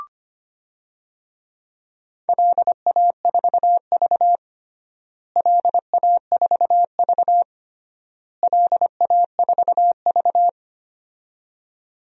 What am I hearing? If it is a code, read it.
Morse code, heard as a keyed tone, about 25 wpm, 700 Hz. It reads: LA4V LA4V LA4V
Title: DXCC-100 Call Signs - 4 Characters - Repeated 3x - 25wpm